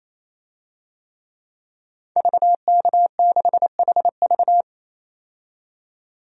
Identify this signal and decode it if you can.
Morse code, heard as a keyed tone, about 28 wpm, 700 Hz. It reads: VK6HV
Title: DXCC-100 Call Signs - 5 Characters - 28wpm